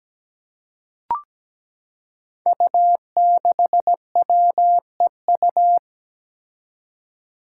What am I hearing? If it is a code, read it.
Morse code, heard as a keyed tone, about 17 wpm, 700 Hz. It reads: U6WEU
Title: Five Challenging Letter-Number Combinations 17wpm